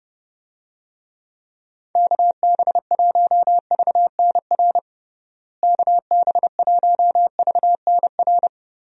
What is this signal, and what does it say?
Morse code, heard as a keyed tone, about 30 wpm, 700 Hz. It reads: KB1VNR KB1VNR
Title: USA Call Signs - 6 Characters - Repeated 3x - 30wpm